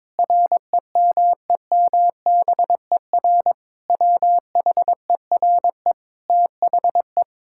Morse code, heard as a keyed tone, about 22 wpm, 700 Hz. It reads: REMEMBER WHERE THE
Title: Sentences from Top 1000 Words - Review Part 6 - 22wpm